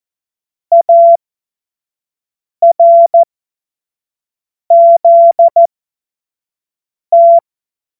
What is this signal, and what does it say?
Morse code, heard as a keyed tone, about 14 wpm, spaced as for 5 wpm, 680 Hz. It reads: ARZT
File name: Morsekurs / Witze / Witz 27. Januar 2026 25 F70